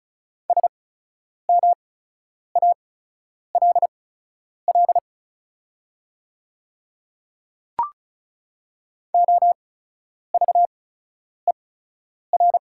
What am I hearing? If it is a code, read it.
Morse code, heard as a keyed tone, about 34 wpm, spaced as for 10 wpm, 690 Hz. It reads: SMALL OVER
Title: 8x Character Spacing - Top 200 Words - 35wpm